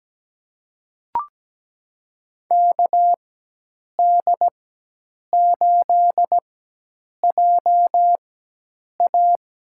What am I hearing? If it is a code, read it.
Morse code, heard as a keyed tone, about 17 wpm, spaced as for 8 wpm, 700 Hz. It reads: KD8JA
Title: USA Call Signs - 6 Characters - 4x Character Spacing - 17wpm